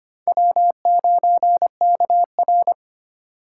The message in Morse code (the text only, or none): W9KR